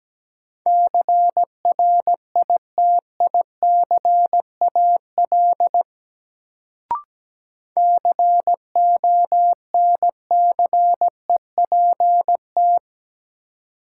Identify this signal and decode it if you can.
Morse code, heard as a keyed tone, about 17 wpm, 700 Hz. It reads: CRITICAL CONCEPT
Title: Sentences from Top 1100 Words - New Word Drill - 17wpm